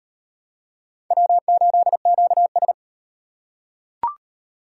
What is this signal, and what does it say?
Morse code, heard as a keyed tone, about 38 wpm, 700 Hz. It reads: W8QS